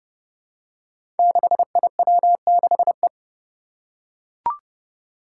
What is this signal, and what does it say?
Morse code, heard as a keyed tone, about 31 wpm, 700 Hz. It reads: 6IW6E
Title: Five Challenging Letter-Number Combinations 30wpm